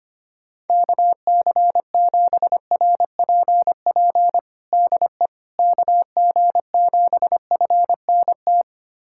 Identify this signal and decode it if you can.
Morse code, heard as a keyed tone, about 25 wpm, 700 Hz. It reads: KC7RPP DE KG7FNT
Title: USA Callsign DE Callsign - 6 Characters - 25wpm